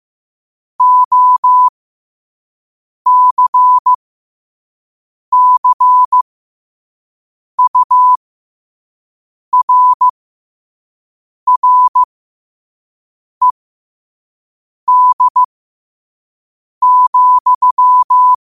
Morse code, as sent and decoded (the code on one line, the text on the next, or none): --- -.-. -.-. ..- .-. .-. -.. --..--
OCCURRED,